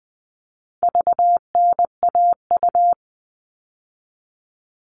Morse code, heard as a keyed tone, about 20 wpm, 700 Hz. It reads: VNAU
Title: All Cut Numbers - 4-Digit 20wpm